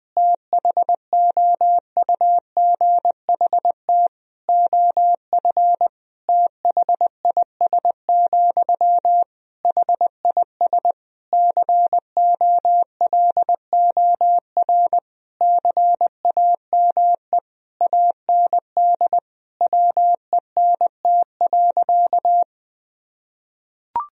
Morse code, heard as a keyed tone, about 20 wpm, 700 Hz. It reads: THOUGHT OF THIS, HIS COLOR CAME AND WENT.
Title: Sentences from Top 1000 Words - Review Part 2 - 20wpm